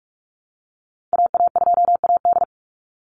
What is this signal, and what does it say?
Morse code, heard as a keyed tone, about 44 wpm, 700 Hz. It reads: AA2AD
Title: Call Signs Repeated 2x - 45wpm